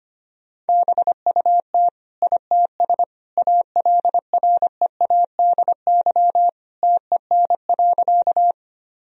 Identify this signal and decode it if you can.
Morse code, heard as a keyed tone, about 25 wpm, 700 Hz. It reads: BUT ITS ALREADY TEN.
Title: Sentences from Top 1000 Words - Review Part 2 - 25wpm